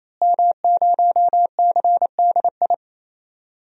Morse code, heard as a keyed tone, about 28 wpm, 700 Hz. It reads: M0CDI
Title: DXCC-100 Call Signs - 5 Characters - Repeated 3x - 28wpm